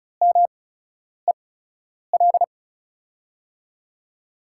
Morse code, heard as a keyed tone, about 36 wpm, spaced as for 10 wpm, 700 Hz. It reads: MEL
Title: USA Call Signs - 6 Characters - 8x Character Spacing - 35wpm